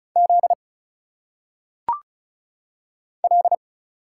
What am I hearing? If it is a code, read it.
Morse code, heard as a keyed tone, about 36 wpm, 700 Hz. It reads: Z L
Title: Single Letters - Phonetic Pronunciation 35wpm